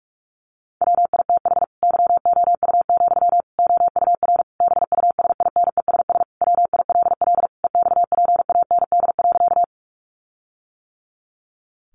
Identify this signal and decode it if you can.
Morse code, heard as a keyed tone, about 44 wpm, 700 Hz. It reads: WITH YOU, OUR BUSINESS WILL EXPAND.
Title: Sentences from Top 1400 Words 45wpm